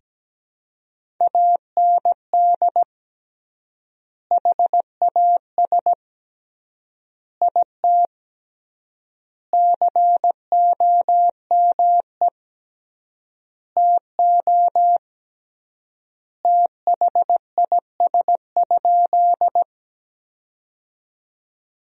Morse code, heard as a keyed tone, about 17 wpm, 700 Hz. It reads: AND HAS IT COME TO THIS?
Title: 3x Word-Spacing - Sentences from Top 100 Words - 17wpm